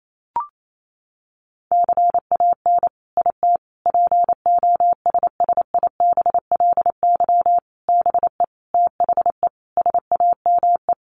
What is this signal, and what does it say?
Morse code, heard as a keyed tone, about 28 wpm, 700 Hz. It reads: CAN IT POSSIBLY BE THE SAME
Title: Sentences from Top 1300 Words 28wpm